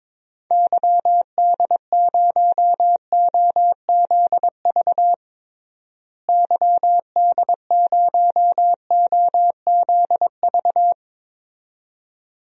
Morse code, heard as a keyed tone, about 22 wpm, 700 Hz. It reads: YD0OZV YD0OZV
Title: DXCC-100 Call Signs - 6 Characters - Repeated 2x - 22wpm